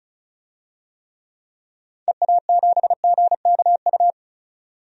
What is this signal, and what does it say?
Morse code, heard as a keyed tone, about 34 wpm, 700 Hz. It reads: EA7GKU
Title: DXCC-100 Call Signs - 6 Characters - 35wpm